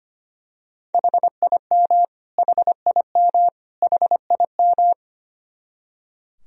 Morse code, heard as a keyed tone, about 25 wpm, 700 Hz. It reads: HIM HIM HIM
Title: Top 100 Words Repeated 25wpm